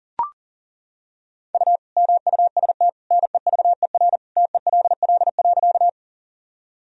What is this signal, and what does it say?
Morse code, heard as a keyed tone, about 39 wpm, 700 Hz. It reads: U MUST NEVER TELL.
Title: Sentences from Top 300 Words and Abbreviations 40wpm